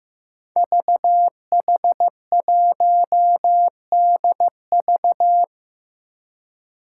Morse code, heard as a keyed tone, about 15 wpm, 700 Hz. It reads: VH1DV
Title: Five Challenging Letter-Number Combinations 15wpm